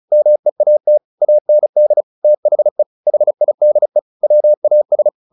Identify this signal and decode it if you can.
Morse code, heard as a keyed tone, about 34 wpm, 600 Hz. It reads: MEAT AND THE HIDE WAS
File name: the old man and the sea 2 35wpm